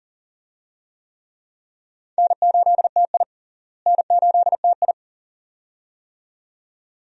Morse code, heard as a keyed tone, about 40 wpm, 700 Hz. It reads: N8TI N8TI
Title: USA Call Signs - 4 Characters - Repeated 2x - 40wpm